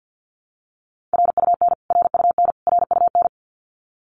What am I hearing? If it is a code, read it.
Morse code, heard as a keyed tone, about 50 wpm, 700 Hz. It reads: RUN RUN RUN
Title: Top 500 Words Repeated 50wpm